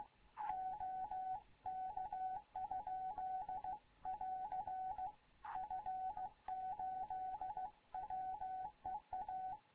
Morse code, heard as a keyed tone, about 15 wpm, 750 Hz, with radio noise. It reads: OK? <AR> F8WEA
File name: Morze-HA2MN / hangp / 2 11 GYAKORLAT